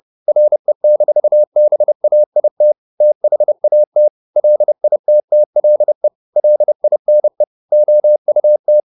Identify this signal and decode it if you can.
Morse code, heard as a keyed tone, about 30 wpm, 600 Hz. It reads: RE-BAIT THAT LITTLE LINE OUT
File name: the old man and the sea / the old man and the sea 1 30wpm